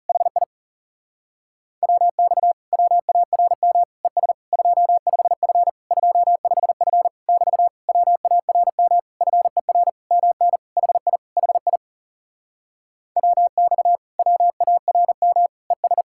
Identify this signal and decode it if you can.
Morse code, heard as a keyed tone, about 38 wpm, 700 Hz. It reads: HI WX WARM ES 25F 25F <BT> WARM FER MN HI HI WX WARM ES